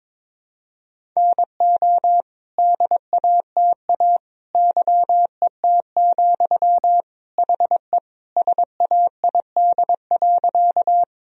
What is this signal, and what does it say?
Morse code, heard as a keyed tone, about 22 wpm, 700 Hz. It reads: NO DATA YET, HE SAID.
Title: Sentences from Top 1000 Words - Review Part 3 - 22wpm